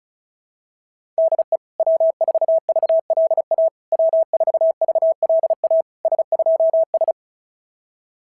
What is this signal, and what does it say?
Morse code, heard as a keyed tone, about 36 wpm, 660 Hz. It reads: DE W4VLA W4VLA S2S
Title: SOTA QSO Practice 35wpm